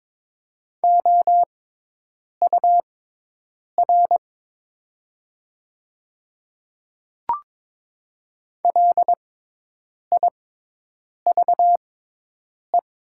Morse code, heard as a keyed tone, about 22 wpm, spaced as for 8 wpm, 700 Hz. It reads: OUR LIVE
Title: 6x Character Spacing - Top 200 Words - 22wpm